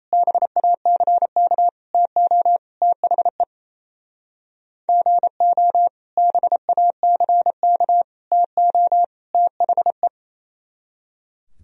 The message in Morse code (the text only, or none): BACK TO THE GO BACK TO THE